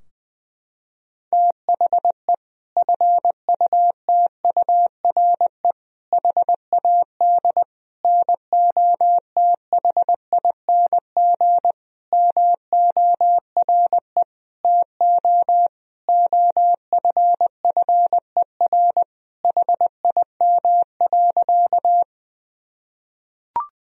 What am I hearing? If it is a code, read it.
Morse code, heard as a keyed tone, about 20 wpm, 700 Hz. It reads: THE FUTURE HAD NOTHING MORE TO OFFER HIM.
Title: Sentences from Top 1000 Words - Review Part 4 - 20wpm